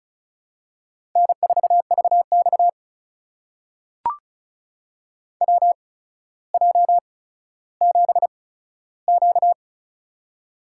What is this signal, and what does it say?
Morse code, heard as a keyed tone, about 36 wpm, spaced as for 15 wpm, 700 Hz. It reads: N4VX WJ7Q
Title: USA Call Signs - 4 Characters - 8x Character Spacing - 35wpm